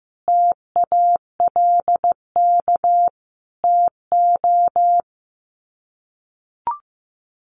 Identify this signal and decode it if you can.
Morse code, heard as a keyed tone, about 15 wpm, 700 Hz. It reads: TALK TO
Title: Sets of 5 Words Encore 15wpm